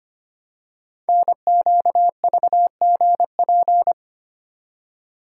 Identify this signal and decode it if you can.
Morse code, heard as a keyed tone, about 25 wpm, 700 Hz. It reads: NQVGP
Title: Code Groups - 5 Letters - 25wpm